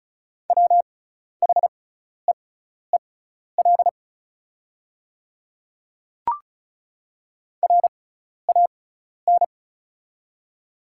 Morse code, heard as a keyed tone, about 34 wpm, spaced as for 12 wpm, 690 Hz. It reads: WHEEL RAN
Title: 6x Character Spacing - Top 500 Words - 35wpm